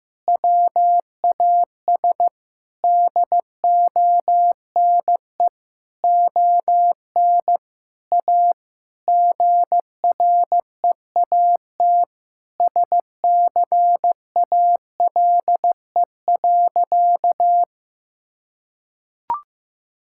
Morse code, heard as a keyed tone, about 15 wpm, 700 Hz. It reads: WAS DONE ON A GREAT SCALE.
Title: Sentences from Top 1000 Words - Review Part 5 - 15wpm